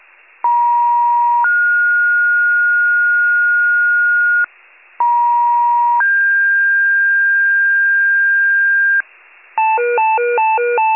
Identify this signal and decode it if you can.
Two-tone page, heard as a one-second tone, then a three-second tone, about 950 then 1430 Hz, about 950 then 1600 Hz, with noise. It is heard from 0.4 to 9.0 s.
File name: wcalert